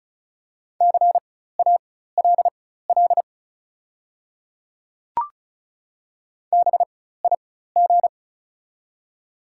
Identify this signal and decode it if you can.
Morse code, heard as a keyed tone, about 36 wpm, spaced as for 16 wpm, 700 Hz. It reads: CALL BIG
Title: 4x Character Spacing - Top 200 Words - 35wpm